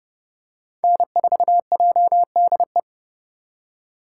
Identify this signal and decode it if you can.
Morse code, heard as a keyed tone, about 30 wpm, 700 Hz. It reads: N4JDE
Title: USA Call Signs - 8x Character Spacing - 30wpm